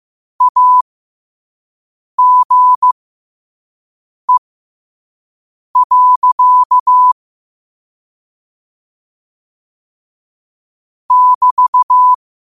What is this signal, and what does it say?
Morse code, heard as a keyed tone, about 15 wpm, spaced as for 5 wpm, 1000 Hz. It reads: AGE. <BT>
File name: qotdfiles / 5wpm